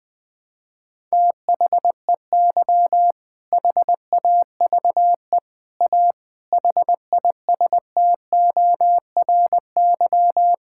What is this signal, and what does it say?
Morse code, heard as a keyed tone, about 20 wpm, 700 Hz. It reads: THEY HAVE A HISTORY
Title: Sentences from Top 1000 Words - Review Part 6 - 20wpm